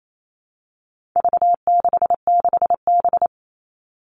Morse code, heard as a keyed tone, about 28 wpm, 700 Hz. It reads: V66B
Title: All Cut Numbers - 4-Digit 28wpm